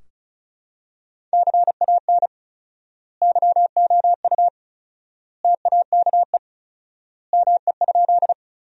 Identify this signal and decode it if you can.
Morse code, heard as a keyed tone, about 36 wpm, 700 Hz. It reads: CAN YOU TAKE ME?
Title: 4x Word-Spacing - Sentences from Top 200 Words - 35wpm